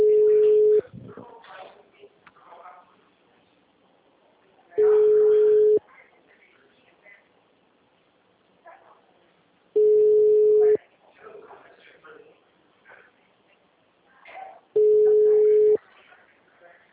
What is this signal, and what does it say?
Ringback tone (Europe, 425 Hz), with speech.